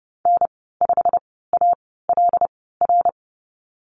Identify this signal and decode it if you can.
Morse code, heard as a keyed tone, about 31 wpm, spaced as for 17 wpm, 700 Hz. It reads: N5ALR
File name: Lesson 15g - Number 5 - Call Signs 30wpm